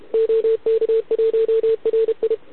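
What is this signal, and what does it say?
Morse code, heard as a keyed tone, about 32 wpm, 440 Hz, with radio noise. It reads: OK1RI
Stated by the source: Contest CW from OK1RI on 80 m during the OKOM DX contest, recorded by a QRP station using a KX3 and a wire antenna.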